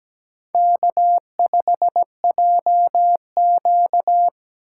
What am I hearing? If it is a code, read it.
Morse code, heard as a keyed tone, about 17 wpm, 700 Hz. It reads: K5JQ
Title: USA Call Signs - 4x Character Spacing - 17wpm